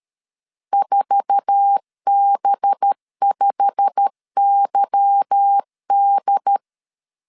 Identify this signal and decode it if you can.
Morse code, heard as a keyed tone, about 13 wpm, 800 Hz. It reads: 4B5YD
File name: CW / DET / 1 9 1500C S C 12wpm 3811